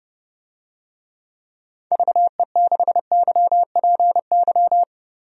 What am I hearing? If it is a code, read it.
Morse code, heard as a keyed tone, about 32 wpm, 700 Hz. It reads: VE6YPY
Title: DXCC-100 Call Signs - 6 Characters - Repeated 2x - 30wpm